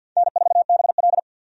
Morse code, heard as a keyed tone, about 53 wpm, 700 Hz. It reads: N4BL
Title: USA Call Signs - 4 Characters - 50wpm